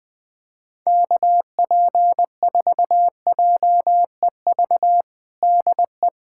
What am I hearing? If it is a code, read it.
Morse code, heard as a keyed tone, about 21 wpm, 700 Hz. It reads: KP4JEV DE
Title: USA Callsign DE Callsign 20wpm